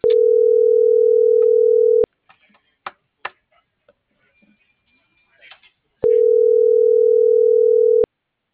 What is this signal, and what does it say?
Ringback tone (North America), 440 + 480 Hz.
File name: CENTRAL MABE / out miriam.soledad - .70111